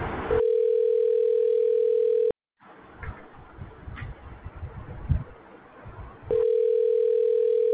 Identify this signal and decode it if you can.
Ringback tone (North America), 440 + 480 Hz.